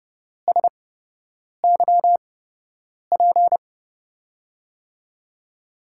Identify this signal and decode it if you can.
Morse code, heard as a keyed tone, about 30 wpm, spaced as for 8 wpm, 700 Hz. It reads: SYP
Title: USA Call Signs - 8x Character Spacing - 30wpm